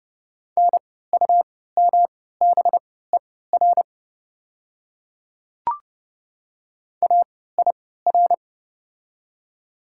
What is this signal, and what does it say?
Morse code, heard as a keyed tone, about 32 wpm, spaced as for 17 wpm, 690 Hz. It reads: NUMBER AIR
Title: 3x Character Spacing - Top 200 Words - 30wpm